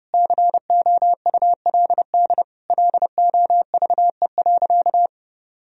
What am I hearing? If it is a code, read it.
Morse code, heard as a keyed tone, about 32 wpm, 700 Hz. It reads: COULD LOVE.
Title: Sentences from Top 1000 Words - Review Part 2 - 30wpm